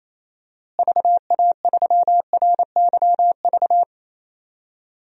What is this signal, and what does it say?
Morse code, heard as a keyed tone, about 27 wpm, 700 Hz. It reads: VA3RYV
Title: DXCC-100 Call Signs - 6 Characters - Repeated 2x - 28wpm